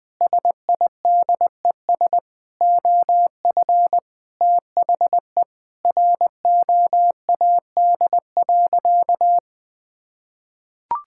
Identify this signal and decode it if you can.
Morse code, heard as a keyed tone, about 21 wpm, 700 Hz. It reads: SIDES OF THE ROAD.
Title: Sentences with 5-Word Phrases - Spoken Beginning - 20wpm